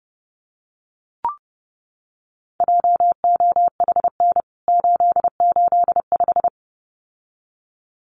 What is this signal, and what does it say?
Morse code, heard as a keyed tone, about 30 wpm, 700 Hz. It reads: JOHN 885
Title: CWT Exchange 30wpm